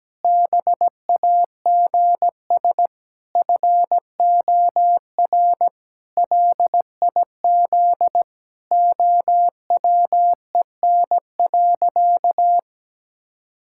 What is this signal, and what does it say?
Morse code, heard as a keyed tone, about 17 wpm, 700 Hz. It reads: BAGS FOR LIZ OWEN.
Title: Pangram Challenge v2 - 17wpm